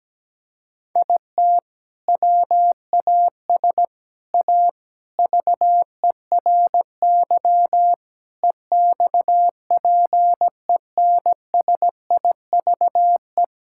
Morse code, heard as a keyed tone, about 17 wpm, 700 Hz. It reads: IT WAS A VERY EXPENSIVE
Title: Sentences from Top 1000 Words - Review Part 6 - 17wpm